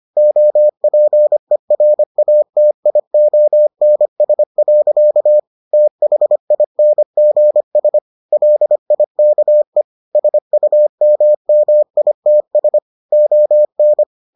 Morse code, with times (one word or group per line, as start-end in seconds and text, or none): 0.2-5.4 OPERATIONS.
5.7-8.0 THINGS
8.3-9.8 LIKE
10.2-12.8 SUMMITS
13.1-14.0 ON